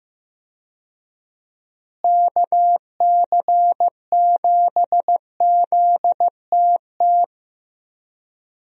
Morse code, heard as a keyed tone, about 15 wpm, 700 Hz. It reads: KC7ZTT
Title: USA Call Signs - 6 Characters - Repeated 3x - 15wpm